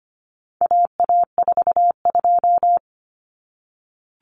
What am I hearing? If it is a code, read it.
Morse code, heard as a keyed tone, about 25 wpm, 700 Hz. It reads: AA42
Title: Common Cut Numbers - 4-Digit 25wpm